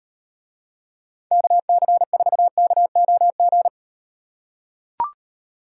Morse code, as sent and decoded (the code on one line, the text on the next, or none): -.- -.-. ....- -.- --- --.
KC4KOG